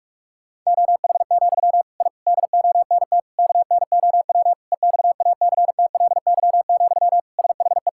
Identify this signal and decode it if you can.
Morse code, heard as a keyed tone, about 41 wpm, 700 Hz. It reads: OH, I DONT KNOW EXACTLY, SHE
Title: Sentences from Top 1000 Words - Review Part 6 - 45wpm